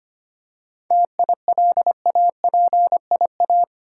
Morse code, heard as a keyed tone, about 25 wpm, 700 Hz. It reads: TILAPIA
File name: practice / 25wpm / Difficult Words with Hints - 25wpm